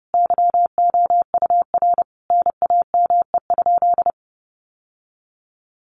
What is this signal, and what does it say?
Morse code, heard as a keyed tone, about 30 wpm, 700 Hz. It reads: YOUR NAME?
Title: Sentences from Top 1200 Words 30wpm